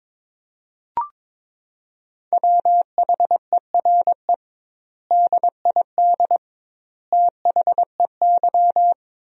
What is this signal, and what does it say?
Morse code, heard as a keyed tone, about 21 wpm, 700 Hz. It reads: WHERE DID THEY
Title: 2x Word-Spacing - Sentences from Top 400 Words - 22wpm